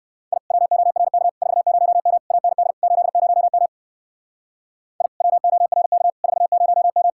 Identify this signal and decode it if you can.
Morse code, heard as a keyed tone, about 55 wpm, 700 Hz. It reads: I WORK 40M AND 80M I WORK 40M
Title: Speed-Racing Common QSO Phrases Encore - 45wpm